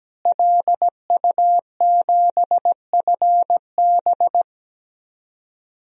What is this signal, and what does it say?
Morse code, heard as a keyed tone, about 17 wpm, 700 Hz. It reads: LU7FB
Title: DXCC-100 Call Signs - 5 Characters - Repeated 3x - 17wpm